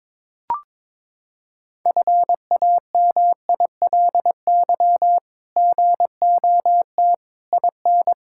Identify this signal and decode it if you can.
Morse code, heard as a keyed tone, about 22 wpm, 700 Hz. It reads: FAMILY GOT IN
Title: Sentences from Top 1000 Words - Review Part 6 - 22wpm